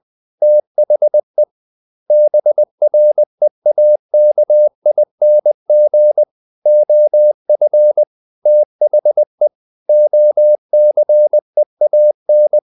Morse code, heard as a keyed tone, about 20 wpm, 600 Hz. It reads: THE BREAKING OF THE OCEAN
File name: the old man and the sea part 2 20wpm